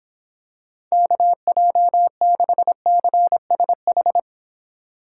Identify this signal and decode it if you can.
Morse code, heard as a keyed tone, about 25 wpm, 700 Hz. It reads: KJ6CSH